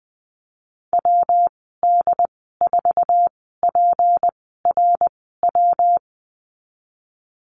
Morse code, heard as a keyed tone, about 19 wpm, spaced as for 14 wpm, 700 Hz. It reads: WD4PRW
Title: 161 Lesson 26g - Letter g - Call Signs 20wpm